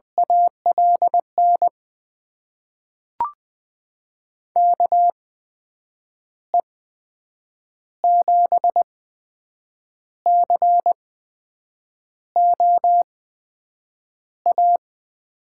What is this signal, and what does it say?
Morse code, heard as a keyed tone, about 20 wpm, spaced as for 7 wpm, 700 Hz. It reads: ALN KE7COA
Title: USA Call Signs - 6 Characters - 8x Character Spacing - 20wpm